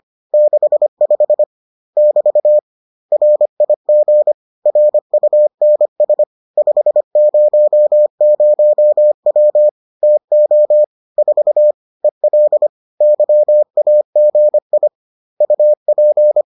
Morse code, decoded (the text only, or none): 65 <BT> RIG RUNS 500W TO 4 EL YAGI UP